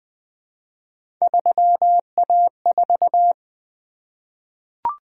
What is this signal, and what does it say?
Morse code, heard as a keyed tone, about 20 wpm, 700 Hz. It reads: 3A4
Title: Three Letter-Number Combinations - 20wpm